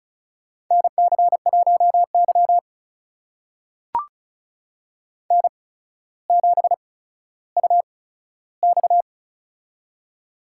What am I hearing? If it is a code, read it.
Morse code, heard as a keyed tone, about 36 wpm, spaced as for 15 wpm, 700 Hz. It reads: NC1Y N7UX
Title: USA Call Signs - 4 Characters - 8x Character Spacing - 35wpm